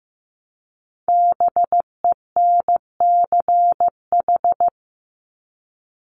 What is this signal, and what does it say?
Morse code, heard as a keyed tone, about 15 wpm, 700 Hz. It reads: BENCH
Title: Top 1251-1500 Words 15wpm